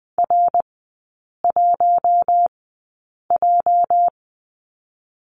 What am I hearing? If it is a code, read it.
Morse code, heard as a keyed tone, about 20 wpm, spaced as for 8 wpm, 690 Hz. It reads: R1J
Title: 205 Lesson 33e - Letter j - Three Characters 20wpm